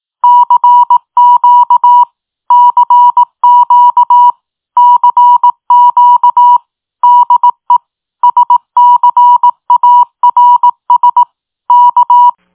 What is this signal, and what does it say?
Morse code, heard as a keyed tone, about 18 wpm, 970 Hz. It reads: CQ CQ CQ DE SCARS K